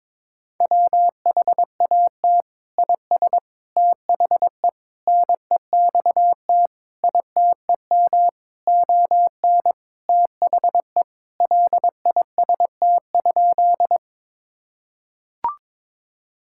Morse code, heard as a keyed tone, about 22 wpm, 700 Hz. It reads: WHAT IS THE NEXT ITEM ON THE LIST?
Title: Sentences from Top 1000 Words - Review Part 6 - 22wpm